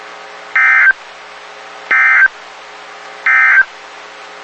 Emergency Alert System (SAME) tones, heard as harsh buzzing bursts. It is heard from 0.4 to 3.7 s.
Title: WXK91 NOAA Weather Radio